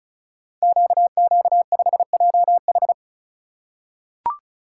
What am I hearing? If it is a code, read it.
Morse code, heard as a keyed tone, about 36 wpm, 690 Hz. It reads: QQ5JH
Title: Code Groups - 5 Letter:Numbers - 35wpm